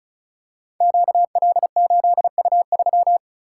Morse code, heard as a keyed tone, about 36 wpm, 700 Hz. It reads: QL8U3
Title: Code Groups - 5 Letter:Numbers - 35wpm